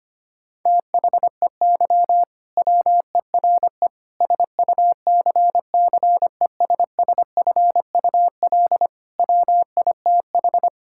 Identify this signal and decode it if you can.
Morse code, heard as a keyed tone, about 25 wpm, 700 Hz. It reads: THEY WERE SUCCESSFUL WITH
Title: Sentences from Top 1000 Words - Review Part 6 - 25wpm